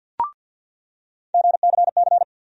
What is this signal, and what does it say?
Morse code, heard as a keyed tone, about 50 wpm, 700 Hz. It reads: GCC
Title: USA Call Sign Postfixes - 3 Letters 50wpm